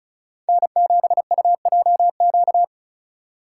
Morse code, heard as a keyed tone, about 34 wpm, 700 Hz. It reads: N7UJQ
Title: USA Call Signs - 5 Characters - 4x Character Spacing - 35wpm